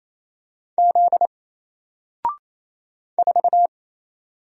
Morse code, heard as a keyed tone, about 27 wpm, 700 Hz. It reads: Z 4
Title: Single Character (Letter:Number:Punctuation:Prosign) 28wpm